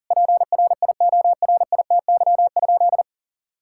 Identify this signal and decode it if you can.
Morse code, heard as a keyed tone, about 40 wpm, 700 Hz. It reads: PRIORITY?
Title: Sentences from Top 1600 Words 40wpm